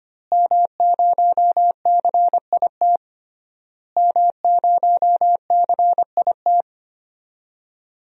Morse code, heard as a keyed tone, about 25 wpm, 700 Hz. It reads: M0CIT M0CIT
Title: DXCC-100 Call Signs - 5 Characters - Repeated 3x - 25wpm